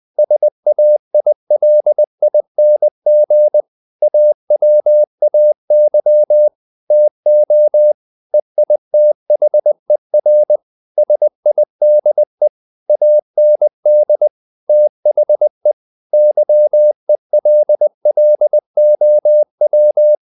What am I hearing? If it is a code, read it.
Morse code, heard as a keyed tone, about 20 wpm, 600 Hz. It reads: SAILING AWAY TO EITHER SIDE AND THE YELLOW
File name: the old man and the sea part 2 20wpm